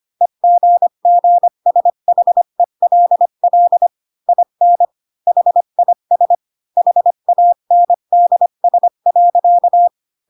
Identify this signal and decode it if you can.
Morse code, heard as a keyed tone, about 25 wpm, 700 Hz. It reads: EGGSHELL IN HIS HANDS.
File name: Pinocchio / Pinocchio ch5 20 25 ik1hge